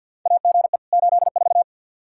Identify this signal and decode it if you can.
Morse code, heard as a keyed tone, about 47 wpm, 700 Hz. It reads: AGE 84
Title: K1USN SST QSO Practice 50wpm